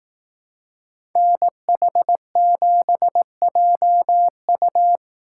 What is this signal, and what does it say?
Morse code, heard as a keyed tone, about 18 wpm, 700 Hz. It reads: NH7JU
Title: Speed Racing - USA Call Signs 15wpm